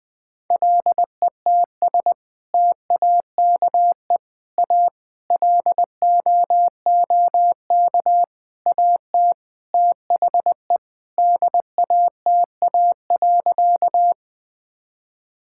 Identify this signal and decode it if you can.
Morse code, heard as a keyed tone, about 21 wpm, 700 Hz. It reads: LETS TAKE A LOOK AT THE DATA.
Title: Sentences with 6-Word Phrases - 20wpm